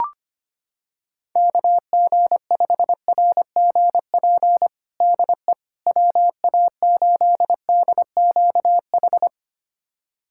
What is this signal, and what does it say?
Morse code, heard as a keyed tone, about 25 wpm, 700 Hz. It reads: KG5RGP DE WA8DQH
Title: USA Callsign DE Callsign 25wpm